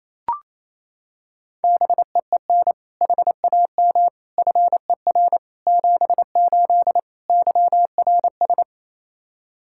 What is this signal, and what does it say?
Morse code, heard as a keyed tone, about 28 wpm, 700 Hz. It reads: BEEN HAM FER 78 YRS
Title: K1USN SST QSO Practice 28wpm